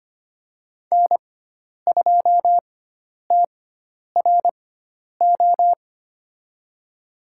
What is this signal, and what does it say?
Morse code, heard as a keyed tone, about 25 wpm, spaced as for 10 wpm, 700 Hz. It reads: N2TRO
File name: Lesson 16g - Letter c - Call Signs 25wpm